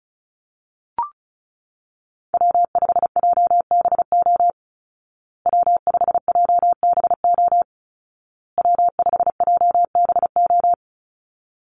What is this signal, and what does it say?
Morse code, heard as a keyed tone, about 34 wpm, 700 Hz. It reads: W5JBO W5JBO W5JBO
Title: Call Signs Repeated 2x - 35wpm